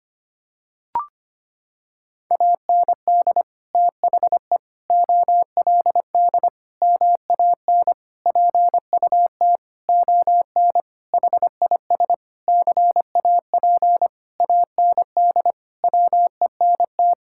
Morse code, heard as a keyed tone, about 25 wpm, 700 Hz. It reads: AND THE OLD MAN PUT ON HIS CAP AND WENT
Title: Sentences from Top 1600 Words 25wpm